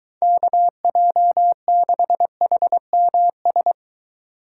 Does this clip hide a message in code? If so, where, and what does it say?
Yes: Morse code from 0.2 to 3.7 s, heard as a keyed tone, about 23 wpm, 700 Hz. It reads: KJ6HMS